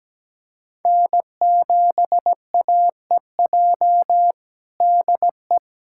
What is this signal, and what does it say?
Morse code, heard as a keyed tone, about 17 wpm, 700 Hz. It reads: N7AEJ DE
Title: USA Callsign DE Callsign 17wpm